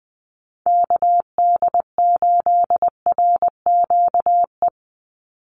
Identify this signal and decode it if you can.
Morse code, heard as a keyed tone, about 20 wpm, 700 Hz. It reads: KD8RQE
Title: Call Signs Repeated 2x - 20wpm